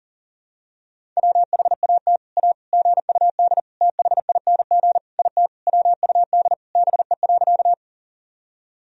Morse code, heard as a keyed tone, about 40 wpm, 700 Hz. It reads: WHAT A GUD THING IT WUD BE.
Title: Sentences from Top 200 Words and Abbreviations 40wpm